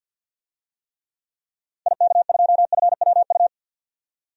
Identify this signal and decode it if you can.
Morse code, heard as a keyed tone, about 47 wpm, 700 Hz. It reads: IK2FWU
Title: DXCC-100 Call Signs - 6 Characters - 50wpm